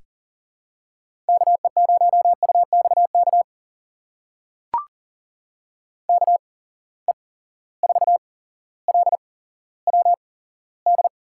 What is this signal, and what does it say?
Morse code, heard as a keyed tone, about 40 wpm, spaced as for 17 wpm, 700 Hz. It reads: KE0UXK KE4LWD
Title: USA Call Signs - 8x Character Spacing - 40wpm